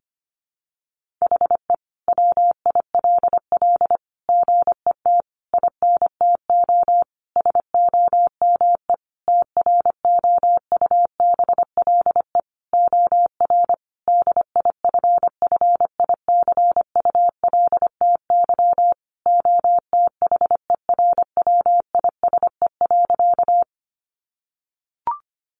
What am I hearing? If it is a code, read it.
Morse code, heard as a keyed tone, about 25 wpm, 700 Hz. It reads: HE WILL GET INTO SOME TROUBLE OR DIFFICULTY OTHERWISE.
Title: Sentences from Top 1500 Words 25wpm